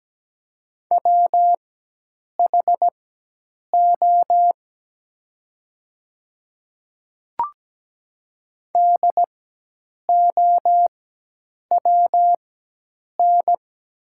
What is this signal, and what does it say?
Morse code, heard as a keyed tone, about 17 wpm, spaced as for 8 wpm, 700 Hz. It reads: WHO DOWN
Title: 4x Character Spacing - Top 200 Words - 17wpm